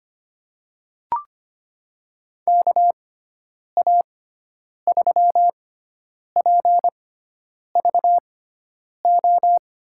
Morse code, heard as a keyed tone, about 25 wpm, spaced as for 9 wpm, 700 Hz. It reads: KA3PVO
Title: USA Call Signs - 6 Characters - 6x Character Spacing - 25wpm